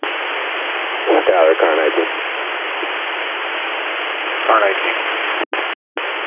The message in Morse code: none

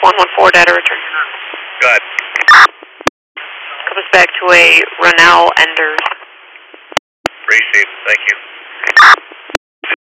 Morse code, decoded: none

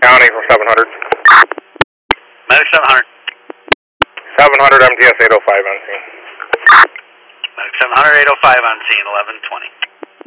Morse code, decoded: none

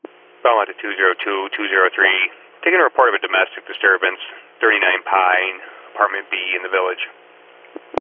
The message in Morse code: none

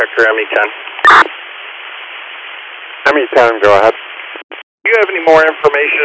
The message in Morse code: none

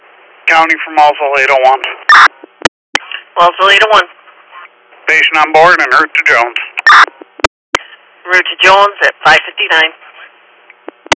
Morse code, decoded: none

